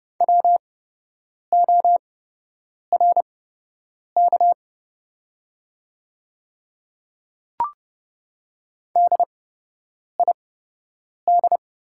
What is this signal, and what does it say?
Morse code, heard as a keyed tone, about 30 wpm, spaced as for 8 wpm, 700 Hz. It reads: WORK DID
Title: 8x Character Spacing - Top 200 Words - 30wpm